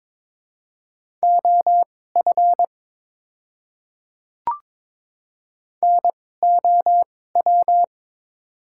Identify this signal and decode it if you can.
Morse code, heard as a keyed tone, about 22 wpm, spaced as for 16 wpm, 700 Hz. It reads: OF NOW
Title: 2x Character Spacing - Top 100 Words - 22wpm